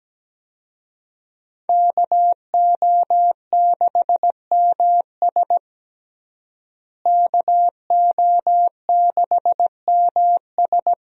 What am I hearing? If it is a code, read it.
Morse code, heard as a keyed tone, about 17 wpm, 700 Hz. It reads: KO6MS KO6MS
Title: USA Call Signs - 5 Characters - Repeated 2x - 17wpm